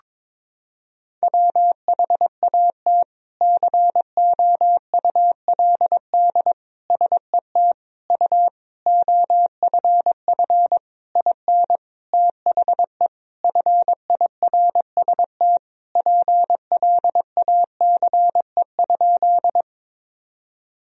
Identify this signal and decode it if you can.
Morse code, heard as a keyed tone, about 22 wpm, 700 Hz. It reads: WHAT COULD SET U OFF IN THE FIRST PLACE?
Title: Sentences from Top 200 Words and Abbreviations 22wpm